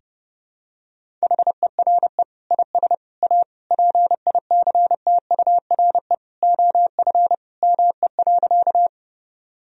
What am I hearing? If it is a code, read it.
Morse code, heard as a keyed tone, about 30 wpm, 700 Hz. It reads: HERE IS A PICTURE OF ME.
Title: Sentences from Top 1000 Words - Review Part 1 - 30wpm